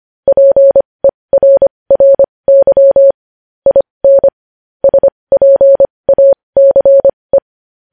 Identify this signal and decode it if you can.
Morse code, heard as a keyed tone, about 25 wpm, spaced as for 20 wpm, 550 Hz. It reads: PERRY IN SPACE
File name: podcast / News headlines-1x-25x20